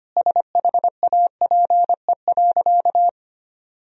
Morse code, heard as a keyed tone, about 25 wpm, 700 Hz. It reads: SHAPE.
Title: Sentences from Top 1000 Words - Review Part 3 - 25wpm